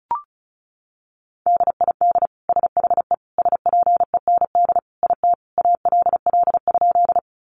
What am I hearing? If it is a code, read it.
Morse code, heard as a keyed tone, about 34 wpm, 700 Hz. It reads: DID SHE SPEND IT ALL?
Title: Sentences from Top 800 Words 35wpm